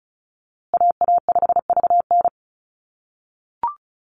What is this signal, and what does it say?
Morse code, heard as a keyed tone, about 34 wpm, 700 Hz. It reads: AA5VN
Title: USA Call Signs Repeated 3x 35wpm